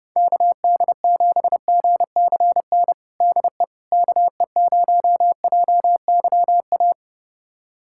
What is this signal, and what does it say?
Morse code, heard as a keyed tone, about 32 wpm, 700 Hz. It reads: KD7GCN DE KE0JYA
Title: USA Callsign DE Callsign 30wpm